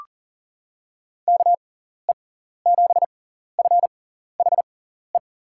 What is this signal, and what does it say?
Morse code, heard as a keyed tone, about 40 wpm, spaced as for 14 wpm, 700 Hz. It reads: KE7FHE
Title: USA Call Signs - 6 Characters - 6x Character Spacing - 40wpm